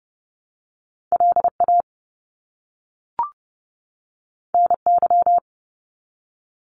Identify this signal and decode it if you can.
Morse code, heard as a keyed tone, about 30 wpm, 700 Hz. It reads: LA NY
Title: US States 30wpm